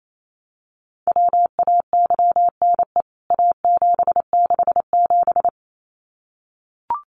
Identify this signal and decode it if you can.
Morse code, heard as a keyed tone, about 28 wpm, 700 Hz. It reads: WAYNE A767
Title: CWT Exchange 28wpm